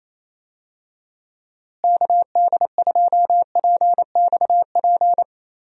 Morse code, heard as a keyed tone, about 28 wpm, 700 Hz. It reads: KD2PXP
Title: USA Call Signs - 6 Characters - Repeated 3x - 28wpm